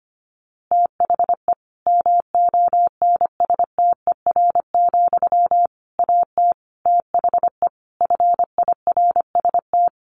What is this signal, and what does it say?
Morse code, heard as a keyed tone, about 25 wpm, 700 Hz. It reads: THE MONSTER, AT THE FIRST
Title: A Sound Of Thunder 25wpm